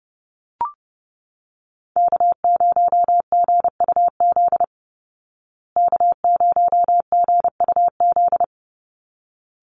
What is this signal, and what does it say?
Morse code, heard as a keyed tone, about 30 wpm, 700 Hz. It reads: K0GUZ K0GUZ
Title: Call Signs Repeated 3x - 30wpm